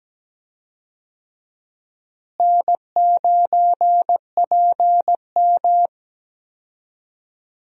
Morse code, heard as a keyed tone, about 17 wpm, 700 Hz. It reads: N9PM
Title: USA Call Signs - 4 Characters - Repeated 2x - 17wpm